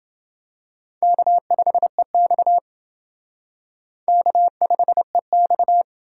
Morse code, heard as a keyed tone, about 28 wpm, 700 Hz. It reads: K5EX K5EX